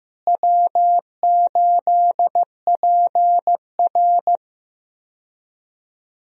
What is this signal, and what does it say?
Morse code, heard as a keyed tone, about 15 wpm, 700 Hz. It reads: W8PR
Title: USA Call Signs - 4 Characters - Repeated 2x - 15wpm